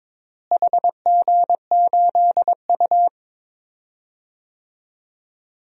Morse code, heard as a keyed tone, about 22 wpm, 700 Hz. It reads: HG8U
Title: DXCC-100 Call Signs - 4 Characters - Repeated 3x - 22wpm